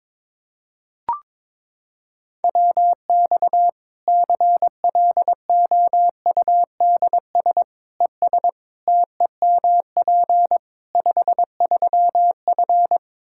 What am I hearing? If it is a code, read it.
Morse code, heard as a keyed tone, about 21 wpm, 700 Hz. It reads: WX CLOUDS ES TEMP 53F
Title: Practice QSO - Exchange 2 - Info - 22wpm